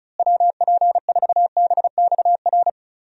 Morse code, heard as a keyed tone, about 34 wpm, 700 Hz. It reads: WP4BXR
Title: USA Call Signs - 6 Characters - Repeated 2x - 35wpm